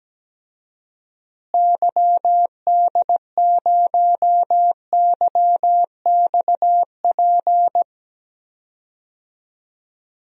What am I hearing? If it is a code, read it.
Morse code, heard as a keyed tone, about 17 wpm, 700 Hz. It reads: YD0YXP
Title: DXCC-100 Call Signs - 6 Characters - Repeated 2x - 17wpm